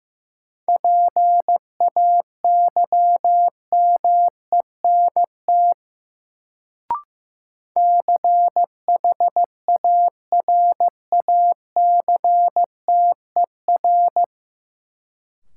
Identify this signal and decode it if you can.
Morse code, heard as a keyed tone, about 15 wpm, 700 Hz. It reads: PAYMENT CHARACTER
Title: Sentences from Top 1200 Words - New Word Drill Alt - 15wpm